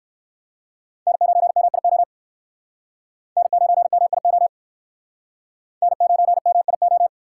Morse code, heard as a keyed tone, about 61 wpm, 700 Hz. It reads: N9MIO N9MIO N9MIO